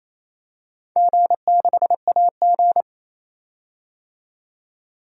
Morse code, heard as a keyed tone, about 28 wpm, 700 Hz. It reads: G6AG
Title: DXCC-100 Call Signs - 4 Characters - 28wpm